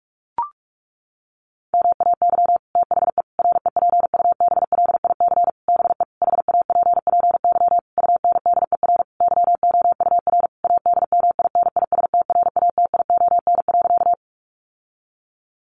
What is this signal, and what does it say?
Morse code, heard as a keyed tone, about 44 wpm, 700 Hz. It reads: MAY THE REPUBLIC BE HAPPY UNDER YOUR ADMINISTRATION.
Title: Sentences from Top 1500 Words 45wpm